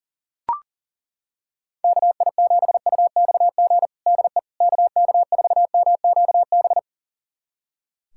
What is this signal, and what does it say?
Morse code, heard as a keyed tone, about 40 wpm, 700 Hz. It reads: KI7UXG DE KK4MQB
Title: USA Callsign DE Callsign - 6 Characters - 40wpm